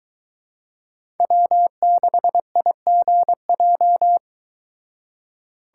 Morse code, heard as a keyed tone, about 23 wpm, 700 Hz. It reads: W6IGJ